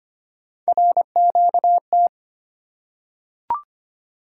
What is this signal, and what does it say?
Morse code, heard as a keyed tone, about 25 wpm, 700 Hz. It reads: RQT
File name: Three Letter-Number Combinations - 25wpm